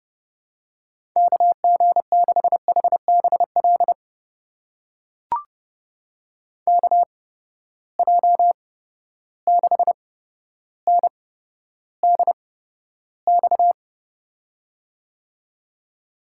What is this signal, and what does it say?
Morse code, heard as a keyed tone, about 30 wpm, spaced as for 13 wpm, 700 Hz. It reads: KG6HBL KJ6NDX
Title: USA Call Signs - 6 Characters - 8x Character Spacing - 30wpm